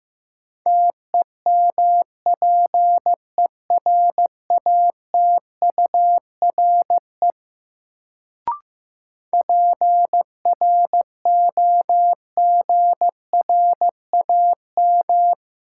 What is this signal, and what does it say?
Morse code, heard as a keyed tone, about 15 wpm, 700 Hz. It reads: TEMPERATURE PROGRAM
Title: Sentences from Top 1500 Words - New Word Drill Alt - 15wpm